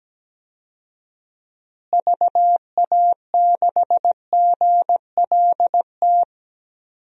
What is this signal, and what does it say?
Morse code, heard as a keyed tone, about 17 wpm, 700 Hz. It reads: VA6GLT
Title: DXCC-100 Call Signs - 6 Characters - Repeated 2x - 17wpm